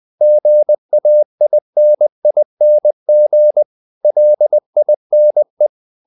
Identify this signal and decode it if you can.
Morse code, heard as a keyed tone, about 20 wpm, 600 Hz. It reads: GAINING LINE
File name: the old man and the sea part 2 20wpm